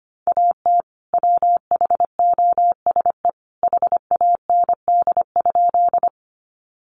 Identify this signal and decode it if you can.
Morse code, heard as a keyed tone, about 25 wpm, 700 Hz. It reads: AT WHOSE HAND?
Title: Sentences from Top 1200 Words 25wpm